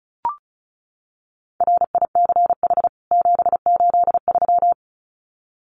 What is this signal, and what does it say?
Morse code, heard as a keyed tone, about 34 wpm, 700 Hz. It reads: RICH 783
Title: CWT Exchange 35wpm